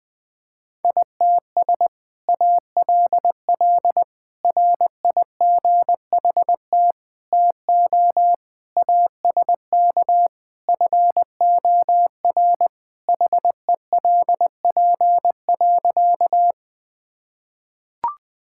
Morse code, heard as a keyed tone, about 20 wpm, 700 Hz. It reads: ITS ALL RIGHT TO ASK FOR HELP.
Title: Sentences with 2-Word Phrases - 20wpm